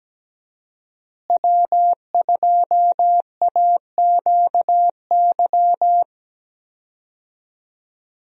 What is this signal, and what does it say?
Morse code, heard as a keyed tone, about 17 wpm, 700 Hz. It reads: W2AQY
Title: USA Call Signs - 5 Characters - Repeated 3x - 17wpm